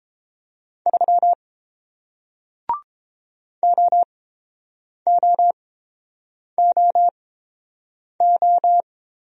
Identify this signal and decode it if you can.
Morse code, heard as a keyed tone, about 30 wpm, spaced as for 7 wpm, 700 Hz. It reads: 3 OOOO